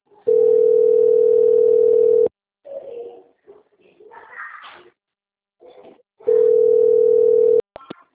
Ringback tone (North America), 440 + 480 Hz.
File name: out- - - - - .61242